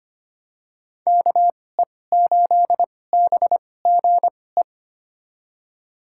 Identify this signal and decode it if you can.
Morse code, heard as a keyed tone, about 25 wpm, spaced as for 18 wpm, 700 Hz. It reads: KE8BGE
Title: USA Call Signs - 6 Characters - 2x Character Spacing - 25wpm